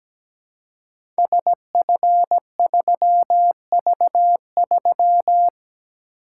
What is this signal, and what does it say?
Morse code, heard as a keyed tone, about 17 wpm, 700 Hz. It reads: SF3V3
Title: Code Groups - 5 Letter:Numbers - 17wpm